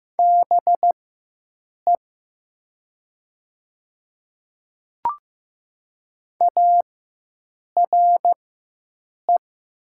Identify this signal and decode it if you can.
Morse code, heard as a keyed tone, about 15 wpm, spaced as for 7 wpm, 700 Hz. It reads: BE ARE
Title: 4x Character Spacing - Top 500 Words - 15wpm